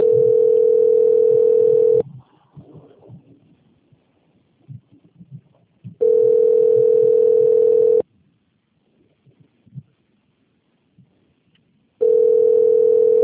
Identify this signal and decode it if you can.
Ringback tone (North America), 440 + 480 Hz.